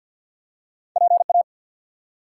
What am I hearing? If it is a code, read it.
Morse code, heard as a keyed tone, about 54 wpm, 700 Hz. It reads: PA
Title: US States and Canadian Provinces 50wpm